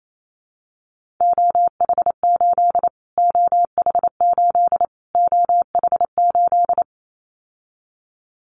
Morse code, heard as a keyed tone, about 28 wpm, 700 Hz. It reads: OH8 OH8 OH8
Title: Callsign Prefixes for Top 25 DXCC - Repeated 3x - 28wpm